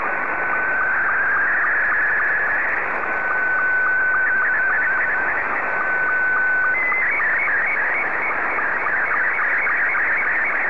SSTV (slow-scan television), Scottie 2.